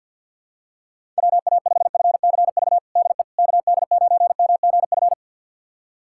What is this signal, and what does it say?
Morse code, heard as a keyed tone, about 51 wpm, 700 Hz. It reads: WA5FCV DE KD9MGF
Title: USA Callsign DE Callsign - 6 Characters - 50wpm